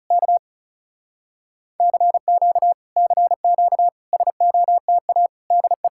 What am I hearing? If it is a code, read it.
Morse code, heard as a keyed tone, about 40 wpm, 700 Hz. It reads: K CQ CQ SOTA DE